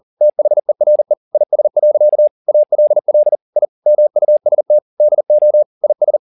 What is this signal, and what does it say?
Morse code, heard as a keyed tone, about 40 wpm, 600 Hz. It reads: THERE IS. ALL I MUST DO IS
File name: the old man and the sea 2 40wpm